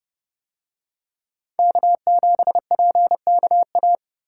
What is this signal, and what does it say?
Morse code, heard as a keyed tone, about 28 wpm, 700 Hz. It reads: K7PKA